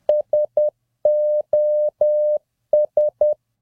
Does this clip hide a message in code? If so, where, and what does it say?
Yes: Morse code from 0.1 to 3.3 s, heard as a keyed tone, about 10 wpm, 600 Hz. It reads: SOS